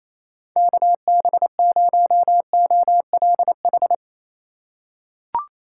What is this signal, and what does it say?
Morse code, heard as a keyed tone, about 28 wpm, 700 Hz. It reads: KB0OLH
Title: USA Call Signs - 4x Character Spacing - 28wpm